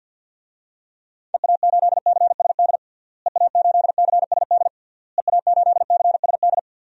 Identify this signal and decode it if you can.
Morse code, heard as a keyed tone, about 52 wpm, 700 Hz. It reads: EA8CSD EA8CSD EA8CSD
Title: DXCC-100 Call Signs - 6 Characters - Repeated 3x - 50wpm